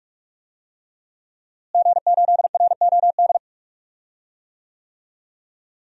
Morse code, heard as a keyed tone, about 42 wpm, 700 Hz. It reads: G8ROD
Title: DXCC-100 Call Signs - 5 Characters - Repeated 3x - 45wpm